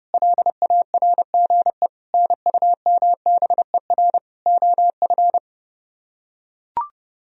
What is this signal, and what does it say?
Morse code, heard as a keyed tone, about 30 wpm, 700 Hz. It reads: LARGE NUMBER OF